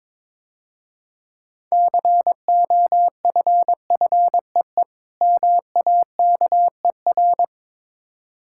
Morse code, heard as a keyed tone, about 22 wpm, 700 Hz. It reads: COFFEE MAKER
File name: Easy Words - Speed-Racing with Hints - 22wpm